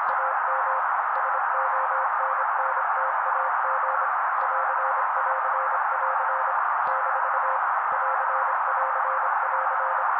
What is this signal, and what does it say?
Morse code, heard as a keyed tone, about 26 wpm, 530 Hz. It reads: AM SONNTAG <AR><AR><AR> - <AR><AR><AR>
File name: daily / 25wpm noise